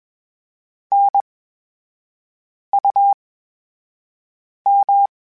Morse code, heard as a keyed tone, about 21 wpm, spaced as for 5 wpm, 800 Hz. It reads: NUM
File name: Special INGs-mp3s / Special INGs-mp3s / VerySlowToFaster Ing @20